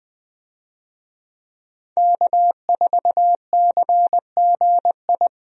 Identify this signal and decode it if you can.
Morse code, heard as a keyed tone, about 20 wpm, 700 Hz. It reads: K4CGI
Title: USA Call Signs - 5 Characters - Repeated 3x - 20wpm